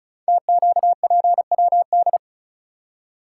Morse code, heard as a keyed tone, about 36 wpm, 700 Hz. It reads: TQPWD
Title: Code Groups - 5 Letters - 35wpm